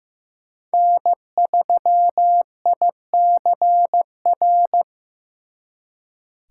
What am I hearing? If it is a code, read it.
Morse code, heard as a keyed tone, about 18 wpm, 700 Hz. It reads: N3ICR
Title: Speed Racing - USA 1x3 Call Signs 15wpm